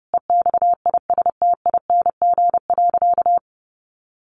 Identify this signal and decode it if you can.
Morse code, heard as a keyed tone, about 30 wpm, 700 Hz. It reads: EXISTING.
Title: Sentences from Top 1500 Words 30wpm